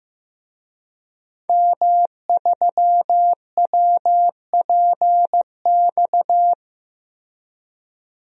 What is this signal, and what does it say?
Morse code, heard as a keyed tone, about 15 wpm, 700 Hz. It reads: M3WPX
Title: DXCC-100 Call Signs - 5 Characters - Repeated 3x - 15wpm